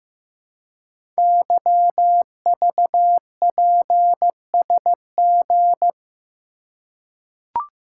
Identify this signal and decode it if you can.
Morse code, heard as a keyed tone, about 15 wpm, 700 Hz. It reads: YVPSG
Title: Code Groups - 5 Letters - 15wpm